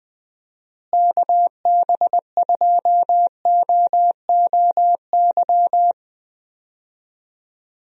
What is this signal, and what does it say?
Morse code, heard as a keyed tone, about 21 wpm, 700 Hz. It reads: KB2OOY
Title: USA Call Signs - 6 Characters - Repeated 2x - 20wpm